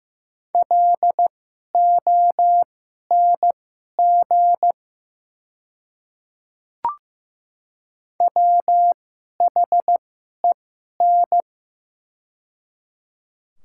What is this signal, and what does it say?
Morse code, heard as a keyed tone, about 15 wpm, spaced as for 11 wpm, 700 Hz. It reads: LONG WHEN
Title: 2x Character Spacing - Top 100 Words - 15wpm